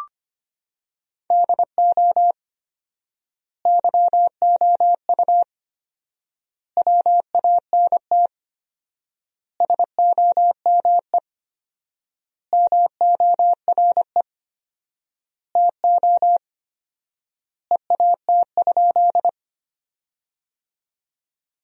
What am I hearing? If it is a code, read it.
Morse code, heard as a keyed tone, about 25 wpm, 700 Hz. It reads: DO YOU WANT SOME MORE TO EAT?
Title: 4x Word-Spacing - Sentences from Top 300 Words - 25wpm